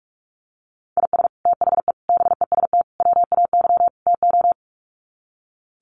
Morse code, heard as a keyed tone, about 44 wpm, 700 Hz. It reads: IS THE BEST WAY TO
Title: Sets of 5 Words Encore 45wpm